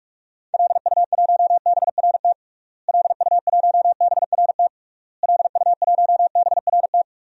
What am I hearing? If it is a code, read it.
Morse code, heard as a keyed tone, about 46 wpm, 700 Hz. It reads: LU1BRT LU1BRT LU1BRT
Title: DXCC-100 Call Signs - 6 Characters - Repeated 3x - 45wpm